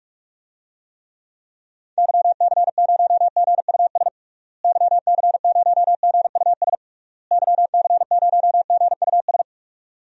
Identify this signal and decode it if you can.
Morse code, heard as a keyed tone, about 46 wpm, 700 Hz. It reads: YC0GUS YC0GUS YC0GUS
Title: DXCC-100 Call Signs - 6 Characters - Repeated 3x - 45wpm